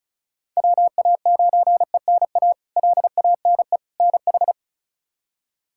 Morse code, heard as a keyed tone, about 36 wpm, 700 Hz. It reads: WA9ENA LANE NH
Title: K1USN SST QSO Practice - 35wpm